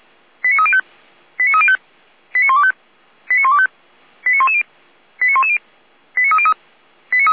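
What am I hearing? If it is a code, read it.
Five-tone selective call (ZVEI), heard as quick runs of five tones, tones of about 70 ms. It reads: 89305 89305 89125 89125 89110 89110 89303
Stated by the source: ZVEI five-tone selective call 89277, received by SDR for feuerwehr-orlamuende fire service.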